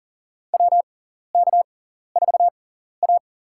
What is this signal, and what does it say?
Morse code, heard as a keyed tone, about 40 wpm, spaced as for 14 wpm, 700 Hz. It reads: WK4A
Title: USA Call Signs - 4 Characters - 6x Character Spacing - 40wpm